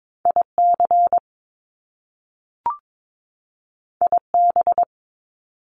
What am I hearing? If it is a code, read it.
Morse code, heard as a keyed tone, about 22 wpm, 700 Hz. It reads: IC IB
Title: Top Two Letter Combinations in Words 22wpm